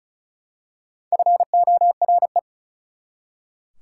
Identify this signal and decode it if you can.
Morse code, heard as a keyed tone, about 36 wpm, 700 Hz. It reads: FORE
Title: Prefixes and Suffixes 35wpm